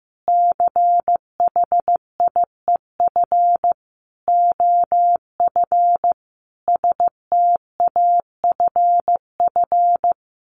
Morse code, heard as a keyed tone, about 15 wpm, 700 Hz. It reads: CHIEF OF STAFF
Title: Sets of 5 Words Encore 15wpm